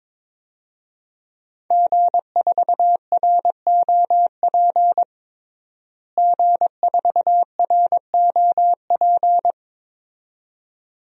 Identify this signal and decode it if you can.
Morse code, heard as a keyed tone, about 22 wpm, 700 Hz. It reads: G4ROP G4ROP
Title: DXCC-100 Call Signs - 5 Characters - Repeated 2x - 22wpm